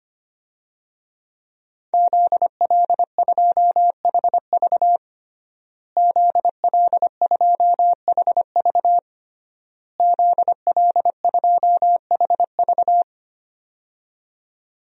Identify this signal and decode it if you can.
Morse code, heard as a keyed tone, about 25 wpm, 700 Hz. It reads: ZL2HV ZL2HV ZL2HV
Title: DXCC-100 Call Signs - 5 Characters - Repeated 3x - 25wpm